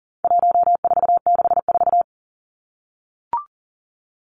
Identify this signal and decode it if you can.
Morse code, heard as a keyed tone, about 38 wpm, 700 Hz. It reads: 1464
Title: Numbers - 4-Digit 40wpm